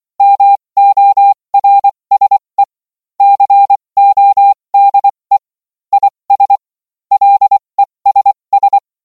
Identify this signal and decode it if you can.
Morse code, heard as a keyed tone, about 24 wpm, spaced as for 20 wpm, 800 Hz. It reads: MORSE CODE IS LESS